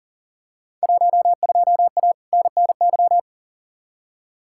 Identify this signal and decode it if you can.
Morse code, heard as a keyed tone, about 39 wpm, 700 Hz. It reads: 12A NNY
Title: Field Day Exchanges Encore - 40wpm